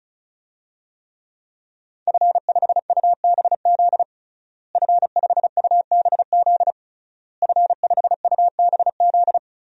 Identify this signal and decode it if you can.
Morse code, heard as a keyed tone, about 34 wpm, 700 Hz. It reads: F5UBZ F5UBZ F5UBZ
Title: DXCC-100 Call Signs - 5 Characters - Repeated 3x - 35wpm